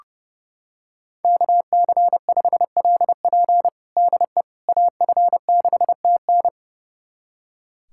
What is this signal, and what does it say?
Morse code, heard as a keyed tone, about 32 wpm, 700 Hz. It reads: KC5LP DE AF6TN
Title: USA Callsign DE Callsign - 5 Characters - 30wpm